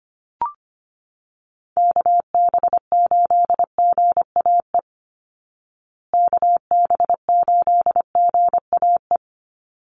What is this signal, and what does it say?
Morse code, heard as a keyed tone, about 25 wpm, 700 Hz. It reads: KB8GAE KB8GAE
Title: Call Signs Repeated 2x - 25wpm